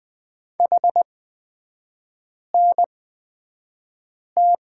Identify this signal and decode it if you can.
Morse code, heard as a keyed tone, about 20 wpm, spaced as for 5 wpm, 690 Hz. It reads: HNT
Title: 229 Lesson 37e - Letter x - Three Characters 20wpm